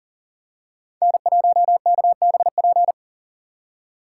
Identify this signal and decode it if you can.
Morse code, heard as a keyed tone, about 40 wpm, 700 Hz. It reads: N1KBP
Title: USA Call Signs - 2x Character Spacing - 40wpm